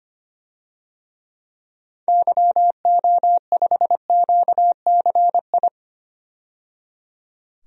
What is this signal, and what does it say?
Morse code, heard as a keyed tone, about 25 wpm, 700 Hz. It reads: YO5QCI
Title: DXCC-100 Call Signs - 6 Characters - 25wpm